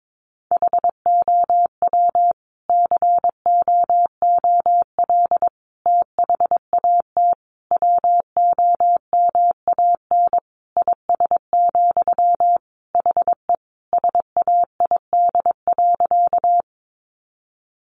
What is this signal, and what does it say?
Morse code, heard as a keyed tone, about 22 wpm, 700 Hz. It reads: HOW COOL THAT WOMAN IS, HE SAID.
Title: Sentences from Top 1200 Words 22wpm